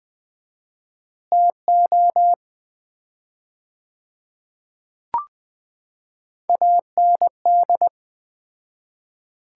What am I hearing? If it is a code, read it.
Morse code, heard as a keyed tone, about 20 wpm, 700 Hz. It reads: TO AND
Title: 4x Word-Spacing - Sets of 5 Words - 20wpm